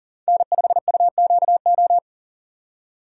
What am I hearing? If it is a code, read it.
Morse code, heard as a keyed tone, about 38 wpm, 700 Hz. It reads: N5UQO
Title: USA Call Signs - 4x Character Spacing - 40wpm